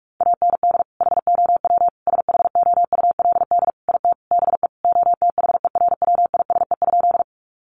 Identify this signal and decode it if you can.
Morse code, heard as a keyed tone, about 44 wpm, 700 Hz. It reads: AND HOW SHOULD IT BE OTHERWISE?
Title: Sentences from Top 1200 Words 45wpm